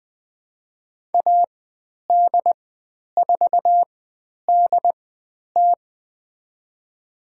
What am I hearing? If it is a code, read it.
Morse code, heard as a keyed tone, about 20 wpm, spaced as for 10 wpm, 700 Hz. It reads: AD4DT
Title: 109 Lesson 18g - Letter m - Call Signs 20wpm